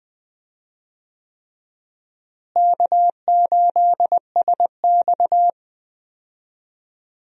Morse code, heard as a keyed tone, about 20 wpm, 700 Hz. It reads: K8SX
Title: USA Call Signs - 4 Characters - Repeated 3x - 20wpm